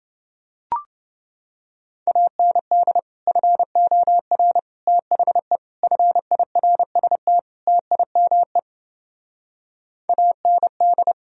Morse code, heard as a keyed tone, about 31 wpm, 700 Hz. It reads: AND FOR THE FIRST TIME AND